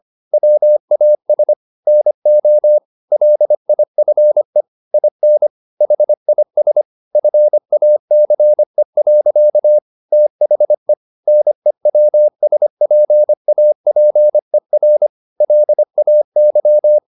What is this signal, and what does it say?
Morse code, heard as a keyed tone, about 25 wpm, 600 Hz. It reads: WAS NO LIFE IN HIS FACE. THE NEWSPAPER LAY
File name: the old man and the sea part 1 25wpm